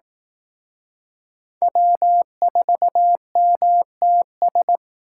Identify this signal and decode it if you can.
Morse code, heard as a keyed tone, about 18 wpm, 700 Hz. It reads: W4MTS
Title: Speed Racing - USA 1x3 Call Signs 15wpm